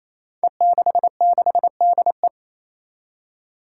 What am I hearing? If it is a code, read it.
Morse code, heard as a keyed tone, about 28 wpm, 700 Hz. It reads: E66DE
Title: Five Challenging Letter-Number Combinations 28wpm